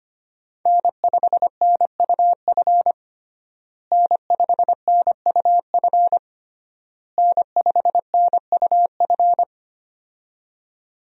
Morse code, heard as a keyed tone, about 25 wpm, 700 Hz. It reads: N5NUF N5NUF N5NUF
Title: USA Call Signs - 5 Characters - Repeated 3x - 25wpm